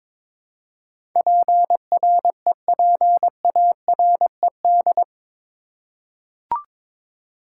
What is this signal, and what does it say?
Morse code, heard as a keyed tone, about 21 wpm, 700 Hz. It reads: PREPARED
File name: practice / 22wpm / Eight Letter Words - 22wpm